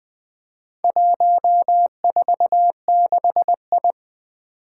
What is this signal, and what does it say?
Morse code, heard as a keyed tone, about 20 wpm, 700 Hz. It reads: 146I
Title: Four Challenging Letter-Number Combinations 20wpm